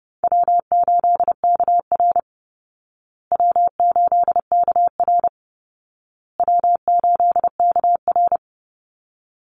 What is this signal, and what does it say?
Morse code, heard as a keyed tone, about 28 wpm, 700 Hz. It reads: W8KR W8KR W8KR
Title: Easy Call Signs Repeated 3x - 30wpm